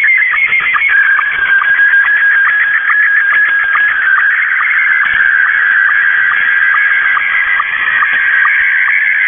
SSTV (slow-scan television), Scottie 1.
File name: mw / 0256 6950u wolverine radio inc sstv